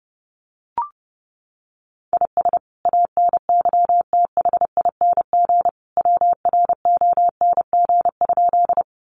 Morse code, heard as a keyed tone, about 30 wpm, 700 Hz. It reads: IS ANYTHING WRONG?
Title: Sentences from Top 1000 Words 30wpm